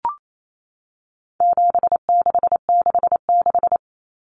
Morse code, heard as a keyed tone, about 28 wpm, 700 Hz. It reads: 7666
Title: Numbers - 4-Digit 28wpm